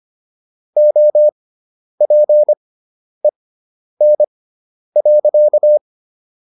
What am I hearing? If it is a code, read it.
Morse code, heard as a keyed tone, about 25 wpm, spaced as for 10 wpm, 600 Hz. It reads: OPEN.